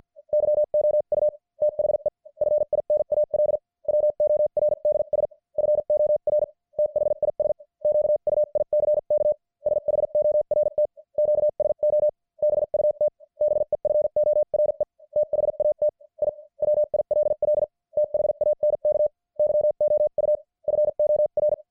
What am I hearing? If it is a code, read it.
Morse code, heard as a keyed tone, about 50 wpm, spaced as for 43 wpm, 600 Hz. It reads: YOU THE FINAL WORDS FOR THIS QUICK SHORT QSO BUT BEFORE THAT I WILL THANK YOU FOR
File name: archived / CWBros